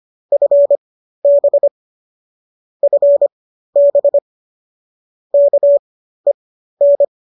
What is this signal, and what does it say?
Morse code, heard as a keyed tone, about 25 wpm, spaced as for 13 wpm, 570 Hz. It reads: FB FB KEN